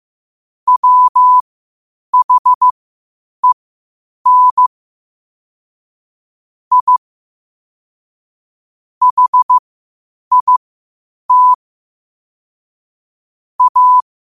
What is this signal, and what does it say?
Morse code, heard as a keyed tone, about 15 wpm, spaced as for 8 wpm, 1000 Hz. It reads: WHEN I HIT A
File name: qotdfiles / 8wpm